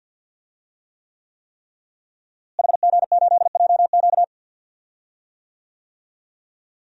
Morse code, heard as a keyed tone, about 51 wpm, 700 Hz. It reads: HG8JQ
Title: DXCC-100 Call Signs - 5 Characters - Repeated 3x - 50wpm